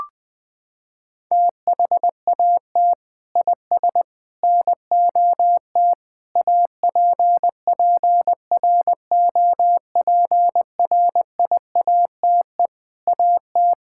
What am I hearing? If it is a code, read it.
Morse code, heard as a keyed tone, about 20 wpm, 700 Hz. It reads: THAT IS NOT APPROPRIATE AT
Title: Sentences from Top 1000 Words - Review Part 5 - 20wpm